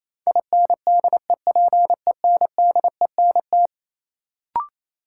Morse code, heard as a keyed tone, about 28 wpm, 700 Hz. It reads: INDEPENDENT
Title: Sentences from Top 900 Words - New Word Drill - 28wpm